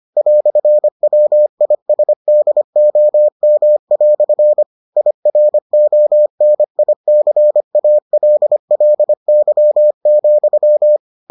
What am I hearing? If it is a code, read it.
Morse code, heard as a keyed tone, about 25 wpm, 600 Hz. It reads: "WISDOM" IRONICALLY,